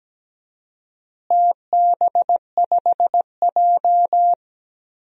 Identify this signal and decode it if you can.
Morse code, heard as a keyed tone, about 17 wpm, 700 Hz. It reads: TB5J
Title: DXCC-100 Call Signs - 4 Characters - Repeated 2x - 17wpm